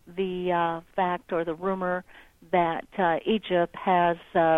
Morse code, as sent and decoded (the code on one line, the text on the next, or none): none